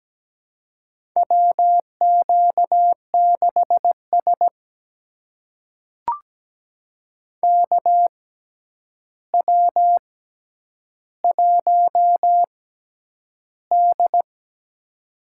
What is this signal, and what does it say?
Morse code, heard as a keyed tone, about 17 wpm, spaced as for 9 wpm, 700 Hz. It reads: WQ6S KW1D
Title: USA Call Signs - 4 Characters - 6x Character Spacing - 17wpm